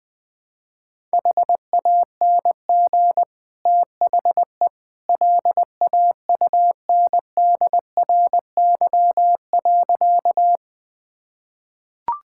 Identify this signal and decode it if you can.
Morse code, heard as a keyed tone, about 20 wpm, 700 Hz. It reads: HANG THE LAUNDRY.
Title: Easy Sentences with Hints - 20wpm